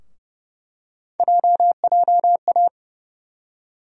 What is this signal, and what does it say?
Morse code, heard as a keyed tone, about 32 wpm, 700 Hz. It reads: JJA
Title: USA Call Sign Postfixes - 3 Letters 30wpm